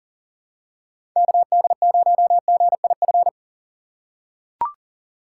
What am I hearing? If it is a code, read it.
Morse code, heard as a keyed tone, about 40 wpm, 700 Hz. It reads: KD0GIF
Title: USA Call Signs - 6 Characters - 3x Character Spacing - 40wpm